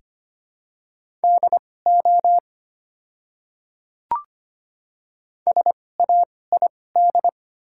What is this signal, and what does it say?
Morse code, heard as a keyed tone, about 25 wpm, spaced as for 18 wpm, 690 Hz. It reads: DO SAID
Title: 2x Character Spacing - Top 100 Words - 25wpm